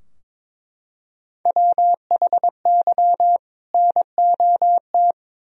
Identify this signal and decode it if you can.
Morse code, heard as a keyed tone, about 22 wpm, 700 Hz. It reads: WHY NOT
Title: Sentences from Top 1000 Words - Review Part 2 - 22wpm